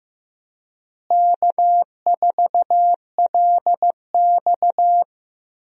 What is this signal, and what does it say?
Morse code, heard as a keyed tone, about 15 wpm, 700 Hz. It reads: K4LX
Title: USA Call Signs - 4 Characters - Repeated 3x - 15wpm